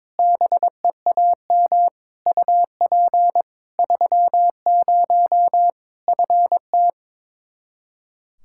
Morse code, heard as a keyed tone, about 22 wpm, 700 Hz. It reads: BEAM UP 30 FT
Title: Practice QSO - Exchange 2 - Info - 22wpm